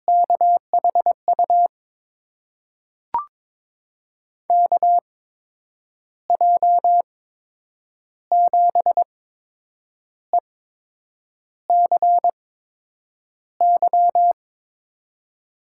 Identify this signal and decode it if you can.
Morse code, heard as a keyed tone, about 21 wpm, spaced as for 8 wpm, 700 Hz. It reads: KHU KJ7ECY
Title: USA Call Signs - 6 Characters - 8x Character Spacing - 22wpm